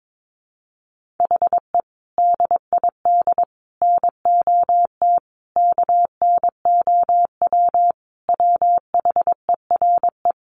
Morse code, heard as a keyed tone, about 22 wpm, 700 Hz. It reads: HE DID NOT KNOW WHERE
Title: Sentences from Top 1500 Words 22wpm